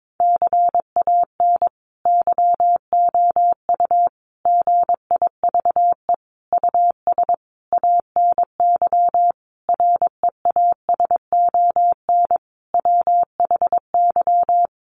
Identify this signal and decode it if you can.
Morse code, heard as a keyed tone, about 22 wpm, 700 Hz. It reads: CAN YOU GIVE US ANY REASON WHY
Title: Sentences from Top 1100 Words 22wpm